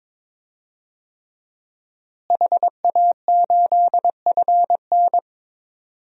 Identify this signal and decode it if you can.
Morse code, heard as a keyed tone, about 22 wpm, 700 Hz. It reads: HA8FN
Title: DXCC-100 Call Signs - 5 Characters - Repeated 3x - 22wpm